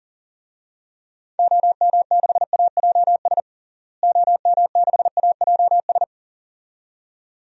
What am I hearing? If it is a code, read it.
Morse code, heard as a keyed tone, about 39 wpm, 700 Hz. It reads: OM6AJS OM6AJS
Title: DXCC-100 Call Signs - 6 Characters - Repeated 2x - 40wpm